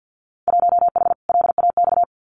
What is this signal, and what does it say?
Morse code, heard as a keyed tone, about 50 wpm, 700 Hz. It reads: 1H LAX
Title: Winter Field Day Exchanges 50wpm